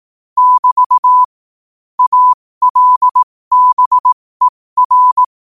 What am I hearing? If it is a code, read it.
Morse code, heard as a keyed tone, about 18 wpm, spaced as for 15 wpm, 1000 Hz. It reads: <BT> ALBER
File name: qotdfiles / 15wpm